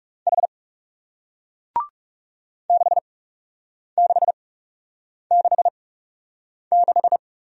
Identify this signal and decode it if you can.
Morse code, heard as a keyed tone, about 41 wpm, spaced as for 8 wpm, 700 Hz. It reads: H 6666